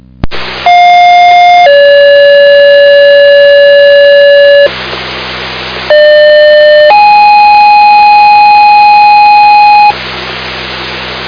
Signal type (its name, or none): two-tone page